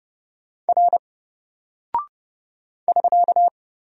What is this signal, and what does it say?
Morse code, heard as a keyed tone, about 31 wpm, 700 Hz. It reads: R <SK>
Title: Single Character (Letter:Number:Punctuation:Prosign) Alt 30wpm